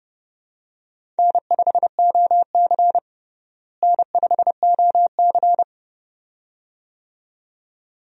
Morse code, heard as a keyed tone, about 30 wpm, 700 Hz. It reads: N5OC N5OC
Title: USA Call Signs - 4 Characters - Repeated 2x - 30wpm